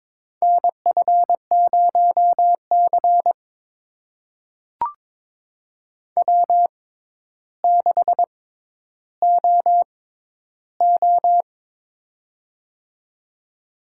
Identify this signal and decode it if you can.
Morse code, heard as a keyed tone, about 22 wpm, spaced as for 11 wpm, 700 Hz. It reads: NF0C W6OO
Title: USA Call Signs - 4 Characters - 6x Character Spacing - 22wpm